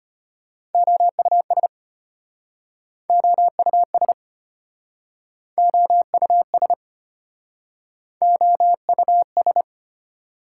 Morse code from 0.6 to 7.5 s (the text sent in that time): OUS OUS OUS